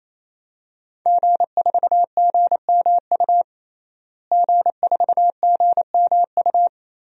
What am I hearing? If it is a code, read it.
Morse code, heard as a keyed tone, about 28 wpm, 700 Hz. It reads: G4GMU G4GMU
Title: DXCC-100 Call Signs - 5 Characters - Repeated 3x - 28wpm